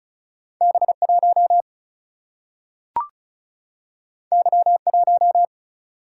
Morse code, heard as a keyed tone, about 36 wpm, 700 Hz. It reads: B 1 Y 1
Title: Single-Letter Single-Number 35wpm